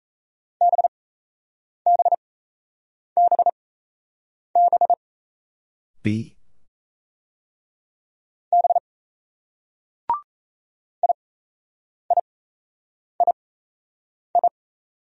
Morse code, heard as a keyed tone, about 33 wpm, spaced as for 8 wpm, 700 Hz. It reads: BBBB B IIII
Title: Speed-Racing Single Letters 28wpm